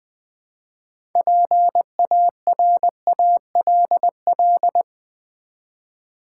Morse code, heard as a keyed tone, about 20 wpm, 700 Hz. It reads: PARALL
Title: Build A Word - 2 Letters At A Time - 8-Letter Words - 20wpm